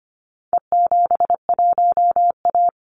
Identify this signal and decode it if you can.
Morse code, heard as a keyed tone, about 25 wpm, 700 Hz. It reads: E71A
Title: Call Signs - 25wpm